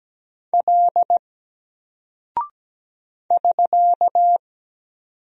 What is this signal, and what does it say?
Morse code, heard as a keyed tone, about 17 wpm, 700 Hz. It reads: L <SK>
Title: Single Character (Letter:Number:Punctuation:Prosign) 17wpm